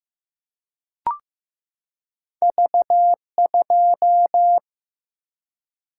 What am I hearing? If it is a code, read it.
Morse code, heard as a keyed tone, about 15 wpm, 700 Hz. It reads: V2
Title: Single-Letter Single-Number 15wpm